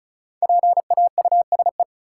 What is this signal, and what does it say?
Morse code, heard as a keyed tone, about 36 wpm, 700 Hz. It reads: PAUSE
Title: Five Letter Words - 35wpm